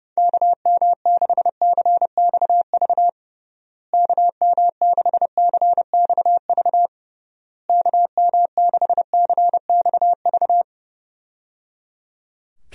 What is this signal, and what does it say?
Morse code, heard as a keyed tone, about 30 wpm, 700 Hz. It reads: KM6CXV KM6CXV KM6CXV
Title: USA Call Signs - 6 Characters - Repeated 3x - 30wpm